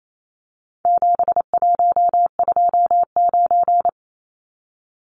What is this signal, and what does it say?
Morse code, heard as a keyed tone, about 28 wpm, 700 Hz. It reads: 7129
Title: Numbers - 4-Digit 28wpm